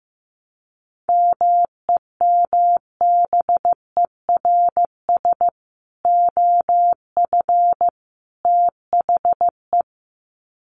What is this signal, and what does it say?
Morse code, heard as a keyed tone, about 15 wpm, 700 Hz. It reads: MEMBERS OF THE
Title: Sets of 3 Words 15wpm